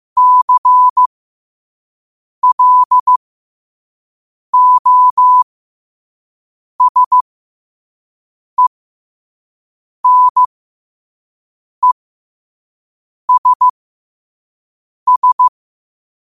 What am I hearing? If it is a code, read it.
Morse code, heard as a keyed tone, about 15 wpm, spaced as for 5 wpm, 1000 Hz. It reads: CLOSENESS
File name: qotdfiles / 5wpm